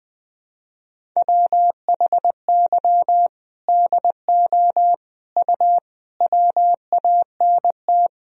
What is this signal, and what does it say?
Morse code, heard as a keyed tone, about 21 wpm, 700 Hz. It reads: WHY DO U WANT
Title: Sentences from Top 200 Words and Abbreviations 20wpm